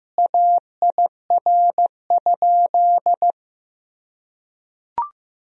Morse code, heard as a keyed tone, about 15 wpm, 700 Hz. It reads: AIR?
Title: Sentences from Top 1000 Words - Review Part 1 - 15wpm